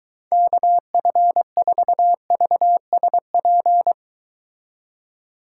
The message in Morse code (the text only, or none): KF4VSP